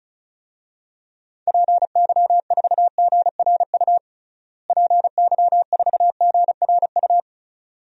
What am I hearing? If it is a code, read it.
Morse code, heard as a keyed tone, about 36 wpm, 700 Hz. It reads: PY4GRU PY4GRU
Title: DXCC-100 Call Signs - 6 Characters - Repeated 2x - 35wpm